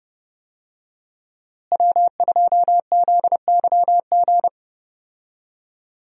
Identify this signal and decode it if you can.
Morse code, heard as a keyed tone, about 30 wpm, 700 Hz. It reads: W2ZYG
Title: USA Call Signs - 5 Characters - Repeated 2x - 30wpm